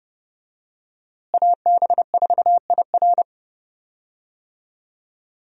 Morse code, heard as a keyed tone, about 30 wpm, 700 Hz. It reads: AB4IR
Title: USA Call Signs - 5 Characters - Repeated 2x - 30wpm